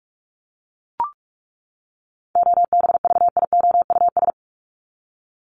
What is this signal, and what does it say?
Morse code, heard as a keyed tone, about 44 wpm, 700 Hz. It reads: OBVIOUS
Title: Top 1501-1750 Words 45wpm